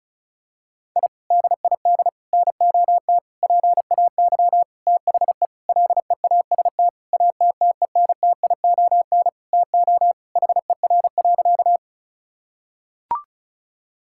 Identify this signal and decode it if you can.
Morse code, heard as a keyed tone, about 36 wpm, 700 Hz. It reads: I DID NOT PAY THE LEAST ATTENTION TO HER.
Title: Sentences from Top 1000 Words - Review Part 6 - 35wpm